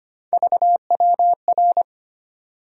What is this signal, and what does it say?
Morse code, heard as a keyed tone, about 25 wpm, 700 Hz. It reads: VWR
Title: USA Call Sign Postfixes - 3 Letters 25wpm